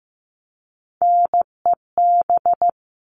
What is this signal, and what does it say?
Morse code, heard as a keyed tone, about 15 wpm, 700 Hz. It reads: NEB
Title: All Cut Numbers - 3-Digit 15wpm